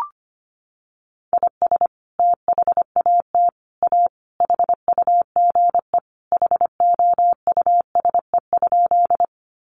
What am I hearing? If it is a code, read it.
Morse code, heard as a keyed tone, about 25 wpm, 700 Hz. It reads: IS THAT A HUGE HOUSE?
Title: Sentences from Top 1100 Words 25wpm